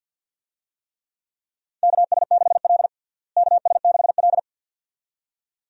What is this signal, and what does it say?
Morse code, heard as a keyed tone, about 52 wpm, 700 Hz. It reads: KS6L KS6L
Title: USA Call Signs - 4 Characters - Repeated 2x - 50wpm